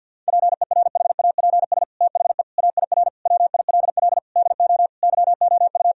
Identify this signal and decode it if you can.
Morse code, heard as a keyed tone, about 48 wpm, 700 Hz. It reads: PERHAPS THE AIR WILL DO YOU
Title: Sentences from Top 1000 Words - Review Part 1 - 50wpm